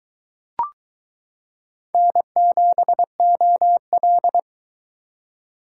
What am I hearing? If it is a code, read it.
Morse code, heard as a keyed tone, about 22 wpm, 700 Hz. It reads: N7OL